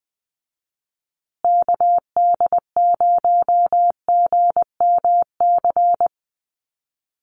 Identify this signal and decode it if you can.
Morse code, heard as a keyed tone, about 20 wpm, 700 Hz. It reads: KD0GMC
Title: USA Call Signs Repeated 3x 20wpm